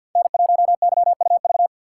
Morse code, heard as a keyed tone, about 46 wpm, 700 Hz. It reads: N1YUV
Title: USA Call Signs - 6x Character Spacing - 50wpm